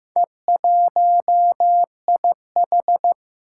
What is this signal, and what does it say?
Morse code, heard as a keyed tone, about 15 wpm, 700 Hz. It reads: E1IH
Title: Four Challenging Letter-Number Combinations 15wpm